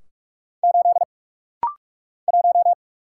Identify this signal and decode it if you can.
Morse code, heard as a keyed tone, about 44 wpm, 700 Hz. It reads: 8 1
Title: Single Letter-Number - Rapid-Fire 45wpm